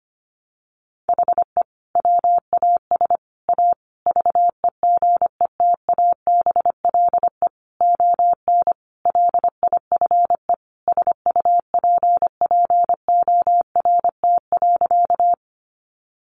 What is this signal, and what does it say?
Morse code, heard as a keyed tone, about 25 wpm, 700 Hz. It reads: HE WAS A VEGETABLE ON LIFE SUPPORT.
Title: Sentences from Top 1300 Words 25wpm